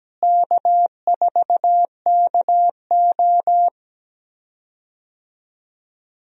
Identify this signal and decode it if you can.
Morse code, heard as a keyed tone, about 17 wpm, 700 Hz. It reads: K4KO
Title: USA Call Signs - 4 Characters - Repeated 2x - 17wpm